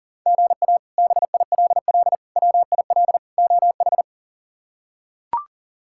Morse code, heard as a keyed tone, about 40 wpm, 700 Hz. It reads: GA BILL WIL OH
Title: K1USN SST QSO Practice - 40wpm